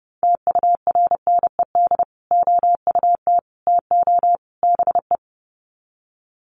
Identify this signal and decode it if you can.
Morse code, heard as a keyed tone, about 30 wpm, 700 Hz. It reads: TURNED OUT TO BE
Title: Sets of 4 Words 30wpm